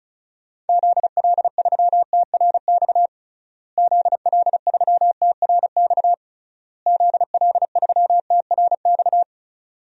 Morse code, heard as a keyed tone, about 36 wpm, 700 Hz. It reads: ZL3TRX ZL3TRX ZL3TRX
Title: DXCC-100 Call Signs - 6 Characters - Repeated 3x - 35wpm